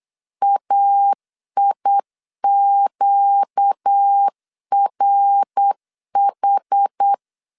Morse code, heard as a keyed tone, about 8 wpm, 800 Hz. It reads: AIQRH
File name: LEZIONI / 6 PKX0 / 6 PK 8wpm 3003